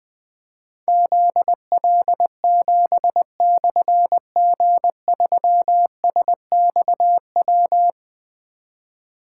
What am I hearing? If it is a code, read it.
Morse code, heard as a keyed tone, about 20 wpm, 700 Hz. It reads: ZL7/G3SXW
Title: Difficult Call Signs 20wpm